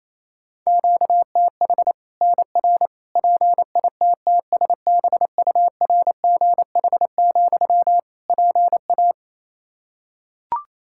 Morse code, heard as a keyed tone, about 28 wpm, 700 Hz. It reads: QTH NR PITTSBURGH, PA
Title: Practice QSO - Exchange 1 - Beginning - 28wpm